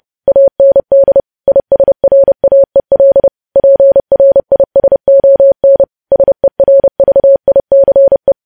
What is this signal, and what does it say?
Morse code, heard as a keyed tone, about 30 wpm, 550 Hz. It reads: AND ISRAEL PRISON SERVICE
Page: news headlines in morse code at 30 wpm